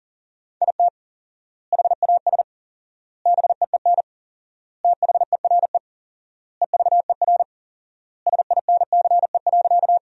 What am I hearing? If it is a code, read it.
Morse code, heard as a keyed tone, about 37 wpm, 700 Hz. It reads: IT HAS BEEN THERE EVER SINCE.
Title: 4x Word-Spacing - Sentences from Top 400 Words - 40wpm